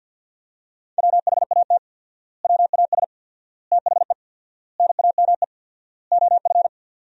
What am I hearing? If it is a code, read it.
Morse code, heard as a keyed tone, about 52 wpm, 700 Hz. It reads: WHAT WAS THE NAME OF
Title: 4x Word-Spacing - Sentences from Top 300 Words - 50wpm